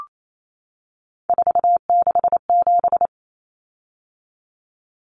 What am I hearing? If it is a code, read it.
Morse code, heard as a keyed tone, about 28 wpm, 700 Hz. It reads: 467
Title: Common Cut Numbers - 3-Digit 28wpm